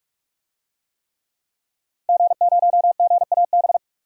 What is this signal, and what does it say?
Morse code, heard as a keyed tone, about 42 wpm, 700 Hz. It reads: G0GAB
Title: DXCC-100 Call Signs - 5 Characters - 45wpm